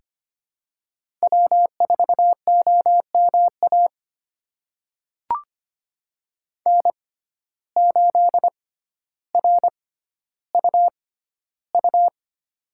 Morse code, heard as a keyed tone, about 25 wpm, spaced as for 13 wpm, 700 Hz. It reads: W4OMA N8RUU
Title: USA Call Signs - 5 Characters - 6x Character Spacing - 25wpm